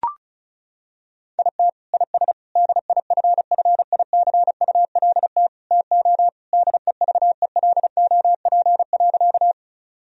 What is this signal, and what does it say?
Morse code, heard as a keyed tone, about 34 wpm, 700 Hz. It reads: IT IS DIFFICULT TO DEVELOP.
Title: Sentences from Top 1000 Words - Review Part 2 - 35wpm